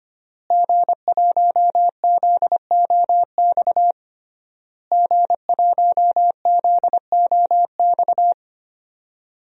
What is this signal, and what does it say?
Morse code, heard as a keyed tone, about 25 wpm, 700 Hz. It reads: G1ZOX G1ZOX
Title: DXCC-100 Call Signs - 5 Characters - Repeated 3x - 25wpm